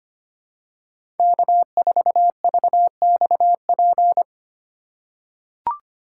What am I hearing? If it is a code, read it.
Morse code, heard as a keyed tone, about 25 wpm, 700 Hz. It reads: K4VXP
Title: USA Call Signs - 5 Characters - 3x Character Spacing - 25wpm